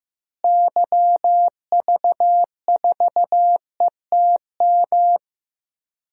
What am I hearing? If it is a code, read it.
Morse code, heard as a keyed tone, about 15 wpm, 700 Hz. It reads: YV4ETM
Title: DXCC-100 Call Signs - 6 Characters - Repeated 3x - 15wpm